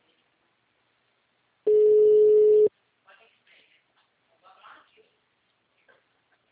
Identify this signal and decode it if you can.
Ringback tone (Europe, 425 Hz), with speech.
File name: out- -2056